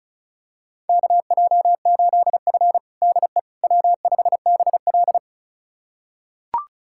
Morse code, heard as a keyed tone, about 34 wpm, 700 Hz. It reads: KJ8F DE W5BL
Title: USA Callsign DE Callsign - 4 Characters - 35wpm